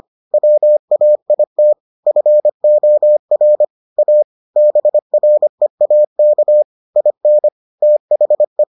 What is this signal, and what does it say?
Morse code, heard as a keyed tone, about 25 wpm, 600 Hz. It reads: WAIT FOR A BREAK IN THE